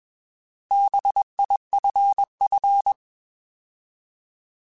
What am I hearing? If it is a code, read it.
Morse code, heard as a keyed tone, about 21 wpm, 800 Hz. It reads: BIFF
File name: Alphabetical / Alphabetical mp3s / Bs @20